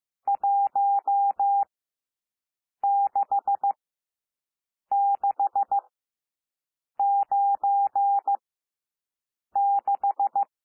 Morse code, heard as a keyed tone, about 15 wpm, spaced as for 6 wpm, 800 Hz. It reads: 16696
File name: cwmp3 / N1 SOUND NUMBERS 15 wpm